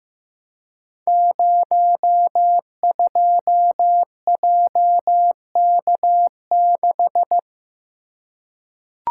Morse code, heard as a keyed tone, about 15 wpm, 700 Hz. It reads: 02JK6
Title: Code Groups - 5 Letter:Numbers - 15wpm